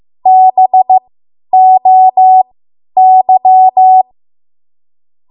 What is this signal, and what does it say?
Morse code, heard as a keyed tone, about 15 wpm, spaced as for 10 wpm, 750 Hz. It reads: BOY
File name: Archive / 10 WPM / 10WPM